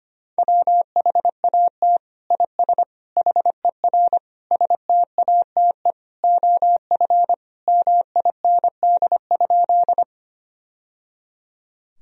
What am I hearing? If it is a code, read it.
Morse code, heard as a keyed tone, about 25 wpm, 700 Hz. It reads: WHAT IS HER STATE OF MIND?
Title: Sentences from Top 1000 Words - Review Part 2 - 25wpm